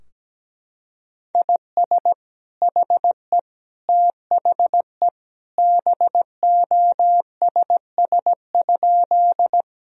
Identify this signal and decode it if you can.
Morse code, heard as a keyed tone, about 17 wpm, 700 Hz. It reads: IS HE THE BOSS?
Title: Sentences from Top 1000 Words - Review Part 4 - 17wpm